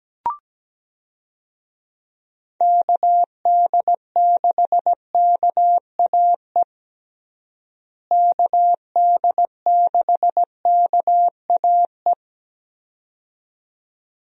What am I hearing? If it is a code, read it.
Morse code, heard as a keyed tone, about 17 wpm, 700 Hz. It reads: KD6KAE KD6KAE
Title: USA Call Signs - 6 Characters - Repeated 2x - 17wpm